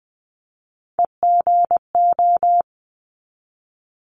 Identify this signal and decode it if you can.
Morse code, heard as a keyed tone, about 20 wpm, 700 Hz. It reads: EGO
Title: 260 Lesson 42e - Words 20-20wpm